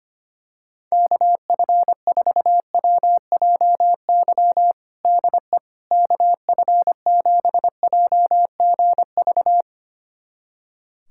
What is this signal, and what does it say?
Morse code, heard as a keyed tone, about 25 wpm, 700 Hz. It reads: KF4WJY DE KF7JGV
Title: USA Callsign DE Callsign 25wpm